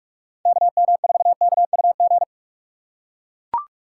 Morse code, heard as a keyed tone, about 43 wpm, 700 Hz. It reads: KM4KUG